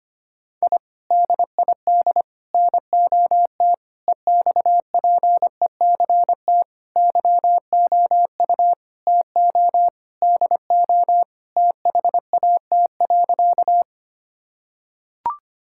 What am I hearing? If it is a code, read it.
Morse code, heard as a keyed tone, about 25 wpm, 700 Hz. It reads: I DID NOT EXPECT YOU TO DO THAT.
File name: practice / 25wpm / Sentences with 3-Word Phrases - 25wpm